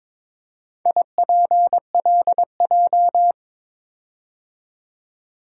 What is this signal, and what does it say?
Morse code, heard as a keyed tone, about 22 wpm, 700 Hz. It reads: IPLJ
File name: practice / 22wpm / Four Letter-Number Combinations - 22wpm